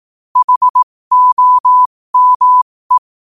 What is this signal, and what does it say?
Morse code, heard as a keyed tone, about 18 wpm, spaced as for 15 wpm, 1000 Hz. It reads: HOME
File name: qotdfiles / 15wpm